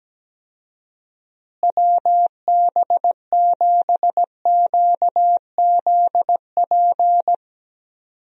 Morse code, heard as a keyed tone, about 17 wpm, 700 Hz. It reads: WB7QZP
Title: USA Call Signs - 6 Characters - Repeated 2x - 17wpm